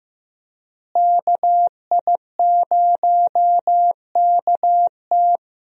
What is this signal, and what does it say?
Morse code, heard as a keyed tone, about 16 wpm, 700 Hz. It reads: KI0KT
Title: USA Call Signs - 5 Characters - Repeated 3x - 15wpm